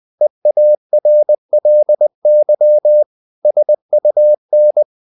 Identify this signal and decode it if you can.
Morse code, heard as a keyed tone, about 20 wpm, 600 Hz. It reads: EARLY SUN
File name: the old man and the sea / the old man and the sea part 1 20wpm